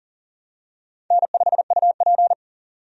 Morse code, heard as a keyed tone, about 38 wpm, 700 Hz. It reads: N5UP
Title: USA Call Signs - 4 Characters - 40wpm